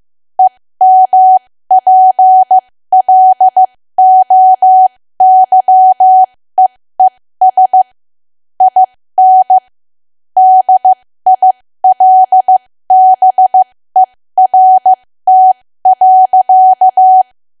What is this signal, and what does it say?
Morse code, heard as a keyed tone, about 15 wpm, spaced as for 13 wpm, 750 Hz. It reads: EMPLOYEES IN DILBERT.
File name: cwa / W1AW-Level2 / 13WPM